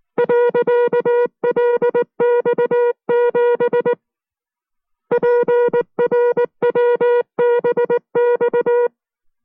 Morse code, heard as a keyed tone, about 19 wpm, 480 Hz. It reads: .LX7 PRWBX